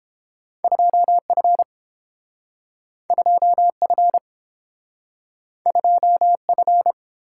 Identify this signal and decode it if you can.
Morse code, heard as a keyed tone, about 32 wpm, 700 Hz. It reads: 2F 2F 2F